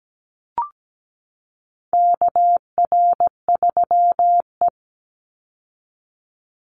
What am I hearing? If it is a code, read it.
Morse code, heard as a keyed tone, about 17 wpm, 700 Hz. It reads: KR3E
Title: Call Signs - 17wpm